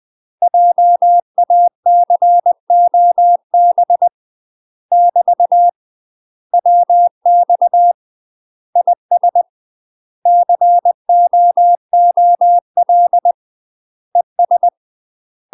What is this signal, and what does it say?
Morse code, heard as a keyed tone, about 20 wpm, 700 Hz. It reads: JACOB <BT> WX IS COOL ES
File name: UK MASTER QSOs TRAINING / 20WPM / UK MASTER QSOs 0007